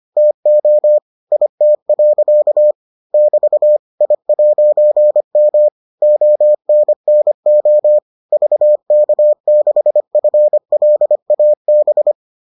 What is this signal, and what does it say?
Morse code, heard as a keyed tone, about 25 wpm, 600 Hz. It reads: TO IT. <BT> I'M ONNO VK6FLAB